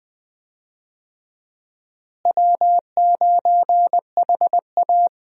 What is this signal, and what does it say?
Morse code, heard as a keyed tone, about 20 wpm, 700 Hz. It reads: W9HA
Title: USA Call Signs - 4 Characters - Repeated 3x - 20wpm